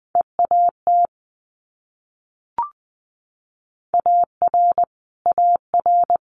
Morse code, heard as a keyed tone, about 20 wpm, 700 Hz. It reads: EAT AR AR
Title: Multi-letter Phonemes Repeated 3x 20wpm